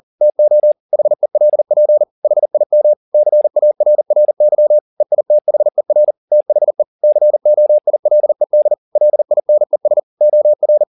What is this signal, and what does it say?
Morse code, heard as a keyed tone, about 40 wpm, 600 Hz. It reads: TO HELP HIM CARRY EITHER THE COILED LINES OR
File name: the old man and the sea 1 40wpm